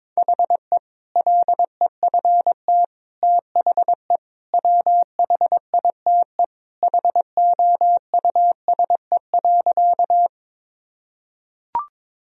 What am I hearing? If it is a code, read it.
Morse code, heard as a keyed tone, about 22 wpm, 700 Hz. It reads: HE LEFT THE WHITE HOUSE.
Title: Sentences from Top 1000 Words - Review Part 1 - 22wpm